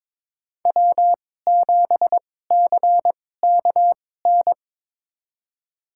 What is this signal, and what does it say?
Morse code, heard as a keyed tone, about 22 wpm, spaced as for 16 wpm, 700 Hz. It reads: W7CKN
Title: USA Call Signs - 5 Characters - 2x Character Spacing - 22wpm